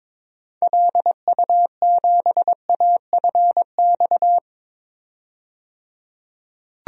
Morse code, heard as a keyed tone, about 22 wpm, 700 Hz. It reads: LU7AFX
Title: DXCC-100 Call Signs - 6 Characters - 22wpm